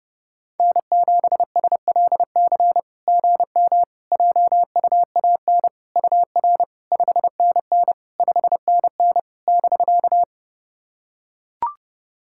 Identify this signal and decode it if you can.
Morse code, heard as a keyed tone, about 32 wpm, 700 Hz. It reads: N7SLC GM JUAN UR 5NN 5NN <BK>
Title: SOTA QSO Practice 30wpm